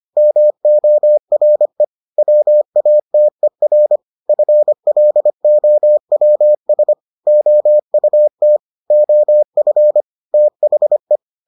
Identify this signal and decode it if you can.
Morse code, heard as a keyed tone, about 25 wpm, 600 Hz. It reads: MORE WATER FLOWS OUT OF THE